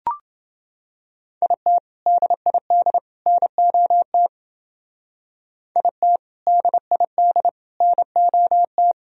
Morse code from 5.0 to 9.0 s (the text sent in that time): IT DID NOT